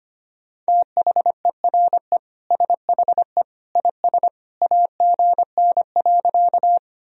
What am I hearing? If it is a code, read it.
Morse code, heard as a keyed tone, about 25 wpm, 700 Hz. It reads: THERE SHE IS AGN.
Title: Sentences from Top 200 Words and Abbreviations 25wpm